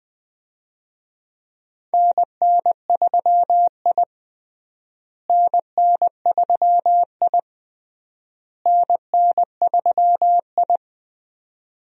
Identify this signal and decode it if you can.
Morse code, heard as a keyed tone, about 21 wpm, 700 Hz. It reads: NN3I NN3I NN3I
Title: USA Call Signs - 4 Characters - Repeated 3x - 20wpm